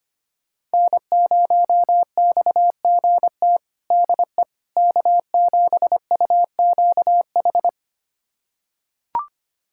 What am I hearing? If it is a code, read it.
Morse code, heard as a keyed tone, about 25 wpm, 700 Hz. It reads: N0XGT DE K7UQH
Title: USA Callsign DE Callsign - 5 Characters - 25wpm